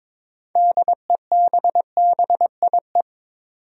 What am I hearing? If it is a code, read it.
Morse code, heard as a keyed tone, about 22 wpm, 700 Hz. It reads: DEBBIE
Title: USA Common Names - 22wpm